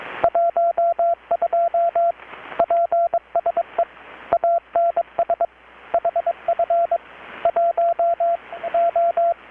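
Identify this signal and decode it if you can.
Morse code, heard as a keyed tone, about 21 wpm, 680 Hz, with radio noise. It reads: 12 PSE ANS HF 1 O
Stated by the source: Morse from KPH's Night of Nights XX maritime radiotelegraphy event, received in Chiapas, Mexico about 3,800 km away on a Kenwood R-600 with a grounded 10 m long-wire antenna.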